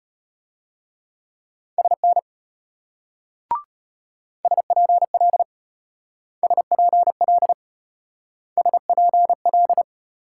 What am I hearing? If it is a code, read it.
Morse code, heard as a keyed tone, about 33 wpm, 700 Hz. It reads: SN SPL SPL SPL